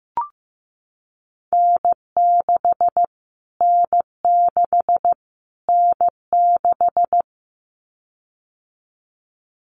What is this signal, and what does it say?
Morse code, heard as a keyed tone, about 15 wpm, 700 Hz. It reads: N6 N6 N6
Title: Callsign Prefixes for Top 25 DXCC - Repeated 3x - 15wpm